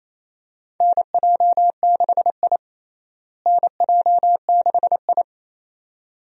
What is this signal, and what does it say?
Morse code, heard as a keyed tone, about 28 wpm, 700 Hz. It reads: NJ6I NJ6I
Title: USA Call Signs - 4 Characters - Repeated 2x - 28wpm